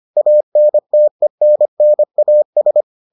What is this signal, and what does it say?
Morse code, heard as a keyed tone, about 25 wpm, 600 Hz. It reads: ANTENNAS